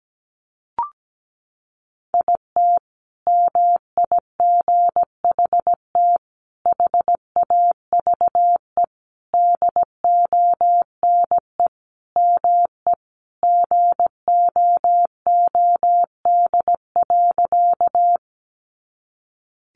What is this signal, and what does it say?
Morse code, heard as a keyed tone, about 17 wpm, 700 Hz. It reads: IT MIGHT HAVE DONE ME GOOD.
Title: Sentences from Top 500 Words 17wpm